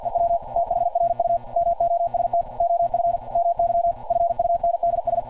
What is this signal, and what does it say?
Morse code, heard as a keyed tone, about 52 wpm, 690 Hz. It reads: EW THRU MOST OF MY FUNDS
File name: archived / CRSnet